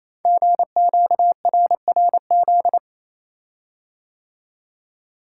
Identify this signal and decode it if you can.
Morse code, heard as a keyed tone, about 29 wpm, 700 Hz. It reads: GQRRZ
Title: Code Groups - 5 Letters - 28wpm